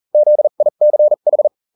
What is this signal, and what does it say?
Morse code, heard as a keyed tone, about 40 wpm, spaced as for 35 wpm, 600 Hz. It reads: ZICH